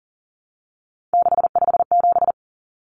Morse code, heard as a keyed tone, about 40 wpm, 700 Hz. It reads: 657
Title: Common Cut Numbers - 3-Digit 40wpm